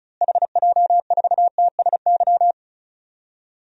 Morse code, heard as a keyed tone, about 34 wpm, 700 Hz. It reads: HJ4TSY
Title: DXCC-100 Call Signs - 6 Characters - Repeated 3x - 35wpm